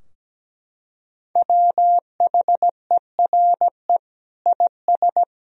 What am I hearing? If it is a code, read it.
Morse code, heard as a keyed tone, about 17 wpm, 700 Hz. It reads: WHERE IS
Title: Sentences from Top 1000 Words - Review Part 4 - 17wpm